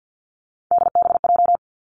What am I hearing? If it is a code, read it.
Morse code, heard as a keyed tone, about 50 wpm, 700 Hz. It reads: DBJ
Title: Three Challenging Letter-Number Combinations 50wpm